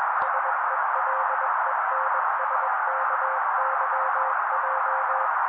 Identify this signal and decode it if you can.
Morse code, heard as a keyed tone, about 21 wpm, 530 Hz. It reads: SELENSKYJ
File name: daily / 20wpm noise